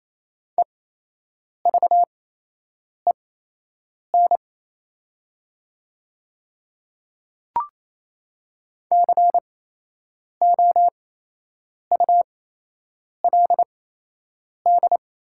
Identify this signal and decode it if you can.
Morse code, heard as a keyed tone, about 27 wpm, spaced as for 8 wpm, 690 Hz. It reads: EVEN COULD
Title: 8x Character Spacing - Top 300 Words - 28wpm